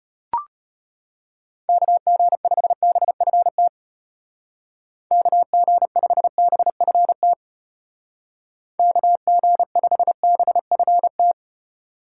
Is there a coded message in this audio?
Yes: Morse code from 1.7 to 11.3 s, heard as a keyed tone, about 33 wpm, 700 Hz. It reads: KG5BFT KG5BFT KG5BFT